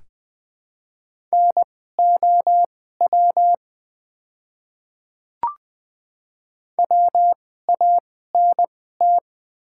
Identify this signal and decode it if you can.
Morse code, heard as a keyed tone, about 20 wpm, spaced as for 15 wpm, 700 Hz. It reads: NOW WANT
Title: 2x Character Spacing - Top 300 Words - 20wpm